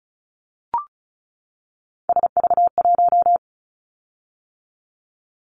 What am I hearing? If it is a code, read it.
Morse code, heard as a keyed tone, about 36 wpm, 700 Hz. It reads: SV1
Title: Callsign Prefixes for Top 50 DXCC - 35wpm